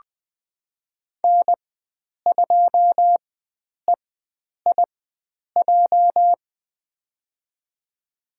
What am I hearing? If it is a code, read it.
Morse code, heard as a keyed tone, about 20 wpm, spaced as for 9 wpm, 700 Hz. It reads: N2EIJ
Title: USA Call Signs - 5 Characters - 4x Character Spacing - 20wpm